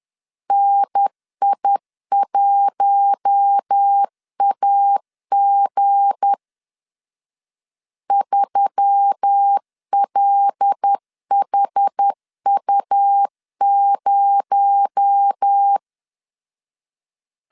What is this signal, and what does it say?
Morse code, heard as a keyed tone, about 10 wpm, 800 Hz. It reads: NI1AG 3LHU0
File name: LEZIONI / 7 GF9Z / 7 GF9 10wpm 2747